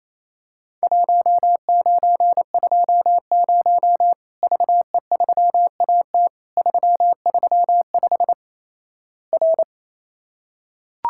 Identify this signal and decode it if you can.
Morse code, heard as a keyed tone, about 28 wpm, 700 Hz. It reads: 1920 VE3AT 335 R
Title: Worked All Europe DX Contest - QTC Exchange - 28wpm